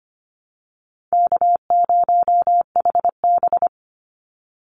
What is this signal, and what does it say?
Morse code, heard as a keyed tone, about 25 wpm, 700 Hz. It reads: K0HB
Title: Call Signs Repeated 2x - 25wpm